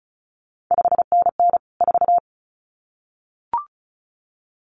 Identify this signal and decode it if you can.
Morse code, heard as a keyed tone, about 36 wpm, 700 Hz. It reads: 5NN 4
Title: CQ WW DX Contest Exchange Encore - 35wpm